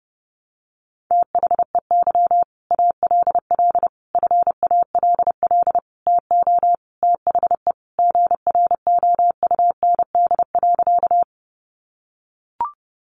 Morse code, heard as a keyed tone, about 30 wpm, 700 Hz. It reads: THEY ALL FALL TO THE GROUND.
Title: Sentences from Top 500 Words 30wpm